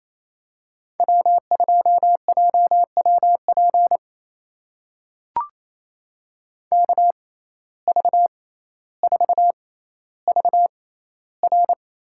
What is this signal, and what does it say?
Morse code, heard as a keyed tone, about 28 wpm, spaced as for 14 wpm, 700 Hz. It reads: W2JWP KV4VR
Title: USA Call Signs - 5 Characters - 6x Character Spacing - 28wpm